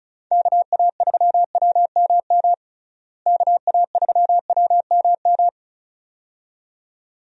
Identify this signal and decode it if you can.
Morse code, heard as a keyed tone, about 36 wpm, 700 Hz. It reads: KA3WMM KA3WMM
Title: USA Call Signs - 6 Characters - Repeated 2x - 35wpm